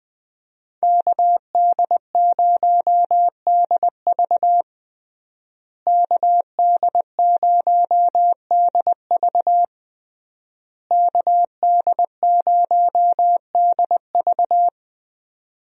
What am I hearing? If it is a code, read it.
Morse code, heard as a keyed tone, about 20 wpm, 700 Hz. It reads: KD0DV KD0DV KD0DV
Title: USA Call Signs - 5 Characters - Repeated 3x - 20wpm